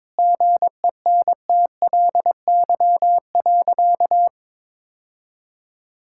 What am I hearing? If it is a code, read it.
Morse code, heard as a keyed tone, about 22 wpm, 700 Hz. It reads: GENTLY.
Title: Sentences from Top 1600 Words 22wpm